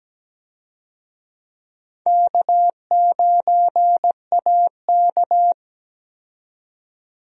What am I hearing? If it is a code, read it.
Morse code, heard as a keyed tone, about 17 wpm, 700 Hz. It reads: K9AK
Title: USA Call Signs - 4 Characters - 17wpm